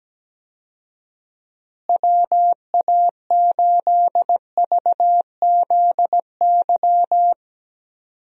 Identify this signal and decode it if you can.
Morse code, heard as a keyed tone, about 17 wpm, 700 Hz. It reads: WA8VZY
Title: USA Call Signs - 6 Characters - Repeated 3x - 17wpm